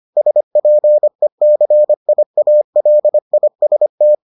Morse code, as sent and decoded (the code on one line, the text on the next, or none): ... .--. -.-. .. .- .-.. .. ... -
SPECIALIST